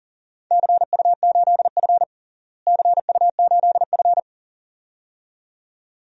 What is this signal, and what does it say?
Morse code, heard as a keyed tone, about 40 wpm, 700 Hz. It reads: CU8F CU8F
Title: DXCC-100 Call Signs - 4 Characters - Repeated 2x - 40wpm